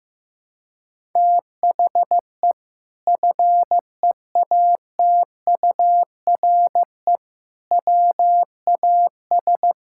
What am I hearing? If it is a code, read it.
Morse code, heard as a keyed tone, about 16 wpm, 700 Hz. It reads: THE FEATURE WAS
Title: Sentences from Top 1000 Words - Review Part 5 - 15wpm